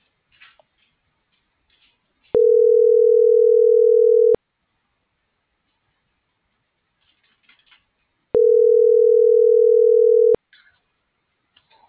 Ringback tone (North America), 440 + 480 Hz.